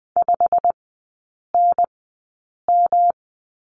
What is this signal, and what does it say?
Morse code, heard as a keyed tone, about 20 wpm, spaced as for 8 wpm, 700 Hz. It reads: 5NM
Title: 107 Lesson 18e - Letter m - Three Characters 20wpm